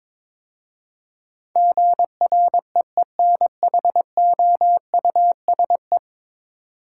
Morse code, heard as a keyed tone, about 22 wpm, 700 Hz. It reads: GREENHOUSE
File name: practice / 22wpm / 10 Letter Words - First 3 Letters Given - 22wpm